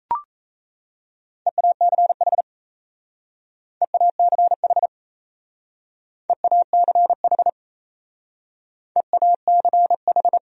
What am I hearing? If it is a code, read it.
Morse code, heard as a keyed tone, about 35 wpm, 700 Hz. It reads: EACH EACH EACH EACH
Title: Speed-Racing Top 100 Words 28wpm